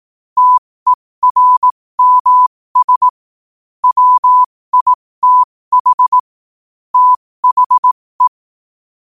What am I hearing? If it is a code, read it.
Morse code, heard as a keyed tone, about 18 wpm, spaced as for 15 wpm, 1000 Hz. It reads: TERMS WITH THE
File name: qotdfiles / 15wpm